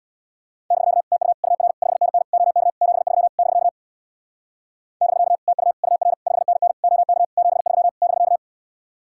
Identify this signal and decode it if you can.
Morse code, heard as a keyed tone, about 65 wpm, 700 Hz. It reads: <BK> TU UR 5NN OK 73 <BK> <BK> TU UR 5NN OK 73 <BK>